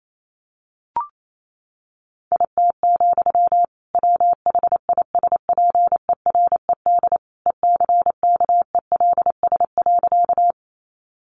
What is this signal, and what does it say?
Morse code, heard as a keyed tone, about 28 wpm, 700 Hz. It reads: IT, WHISPERED ECKELS.
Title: A Sound Of Thunder 28wpm